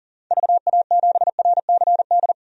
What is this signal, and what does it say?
Morse code, heard as a keyed tone, about 40 wpm, 700 Hz. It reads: VA7RCD
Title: DXCC-100 Call Signs - 6 Characters - 40wpm